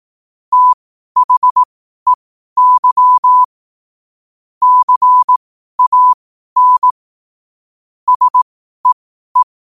Morse code, heard as a keyed tone, about 18 wpm, spaced as for 12 wpm, 1000 Hz. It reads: THEY CAN SEE
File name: qotdfiles / 12wpm